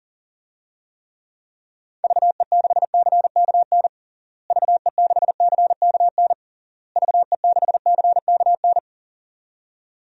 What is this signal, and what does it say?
Morse code, heard as a keyed tone, about 40 wpm, 700 Hz. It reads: VE6CKN VE6CKN VE6CKN
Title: DXCC-100 Call Signs - 6 Characters - Repeated 3x - 40wpm